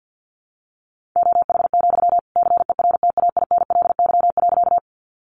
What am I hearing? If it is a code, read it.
Morse code, heard as a keyed tone, about 50 wpm, 700 Hz. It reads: OH, CERTAINLY.
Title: Sentences from Top 900 Words 50wpm